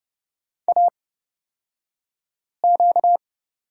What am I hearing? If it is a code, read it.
Morse code, heard as a keyed tone, about 30 wpm, spaced as for 5 wpm, 700 Hz. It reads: AQ
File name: Lesson 38d - Letter q - Two Characters 30wpm